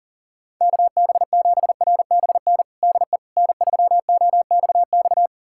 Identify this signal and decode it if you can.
Morse code, heard as a keyed tone, about 40 wpm, 700 Hz. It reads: KB7RBN DE N3OXX
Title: USA Callsign DE Callsign 40wpm